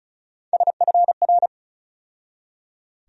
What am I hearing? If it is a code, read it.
Morse code, heard as a keyed tone, about 36 wpm, 700 Hz. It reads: SFR
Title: USA Call Sign Postfixes - 3 Letters 35wpm